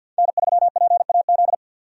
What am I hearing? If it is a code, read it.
Morse code, heard as a keyed tone, about 51 wpm, 700 Hz. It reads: N3PAZ
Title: USA Call Signs - 5 Characters - 50wpm